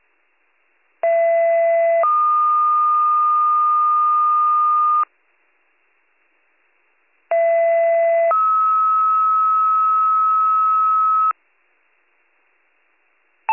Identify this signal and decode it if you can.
Two-tone page, heard as a one-second tone, then a three-second tone, about 670 then 1220 Hz, about 670 then 1290 Hz. It is heard from 1.0 to 11.3 s.